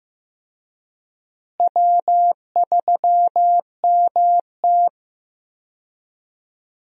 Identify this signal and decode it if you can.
Morse code, heard as a keyed tone, about 15 wpm, 700 Hz. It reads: W3MT
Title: USA Call Signs - 4 Characters - Repeated 3x - 15wpm